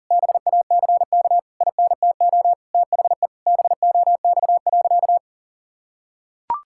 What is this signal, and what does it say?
Morse code, heard as a keyed tone, about 40 wpm, 700 Hz. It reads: BACK INTO THE BOX.
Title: Sentences from Top 1000 Words - Review Part 6 - 40wpm